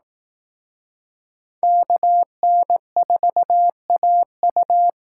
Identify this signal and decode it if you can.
Morse code, heard as a keyed tone, about 18 wpm, 700 Hz. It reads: KN4AU
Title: Speed Racing - USA 2x2 Call Signs 15wpm